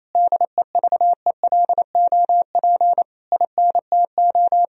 Morse code, heard as a keyed tone, about 28 wpm, 700 Hz. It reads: DEVELOP INTO
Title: Sentences from Top 1000 Words - Review Part 2 - 28wpm